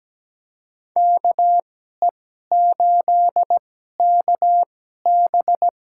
Morse code, heard as a keyed tone, about 17 wpm, spaced as for 12 wpm, 700 Hz. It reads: KE8KB
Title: USA Call Signs - 2x Character Spacing - 17wpm